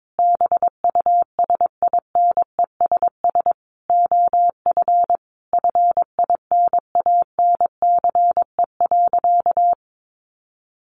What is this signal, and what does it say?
Morse code, heard as a keyed tone, about 22 wpm, 700 Hz. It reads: BUSINESS OF FINANCE.
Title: Sentences from Top 1500 Words 22wpm